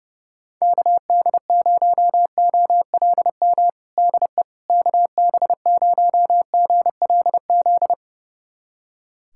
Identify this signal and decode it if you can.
Morse code, heard as a keyed tone, about 30 wpm, 700 Hz. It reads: KD0OLM DE KB0GLZ
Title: USA Callsign DE Callsign - 6 Characters - 30wpm